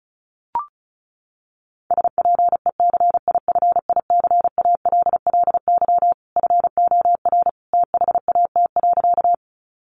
Morse code, heard as a keyed tone, about 34 wpm, 700 Hz. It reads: SPECIFICALLY FOR THAT.
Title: Sentences from Top 1100 Words 35wpm